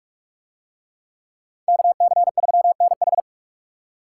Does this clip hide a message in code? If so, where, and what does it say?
Yes: Morse code from 1.7 to 3.2 s, heard as a keyed tone, about 40 wpm, 700 Hz. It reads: KC3NH